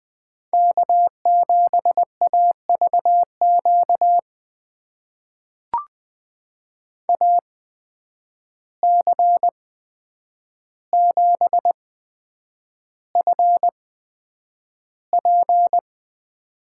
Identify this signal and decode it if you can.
Morse code, heard as a keyed tone, about 20 wpm, spaced as for 9 wpm, 700 Hz. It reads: K7AVQ AC7FP
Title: USA Call Signs - 5 Characters - 8x Character Spacing - 20wpm